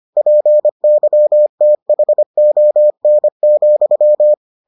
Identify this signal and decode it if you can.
Morse code, heard as a keyed tone, about 25 wpm, 600 Hz. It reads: PYTHON,